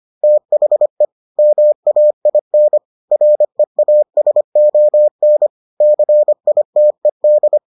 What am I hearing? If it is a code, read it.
Morse code, heard as a keyed tone, about 25 wpm, 600 Hz. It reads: THE MAIN REASON CITED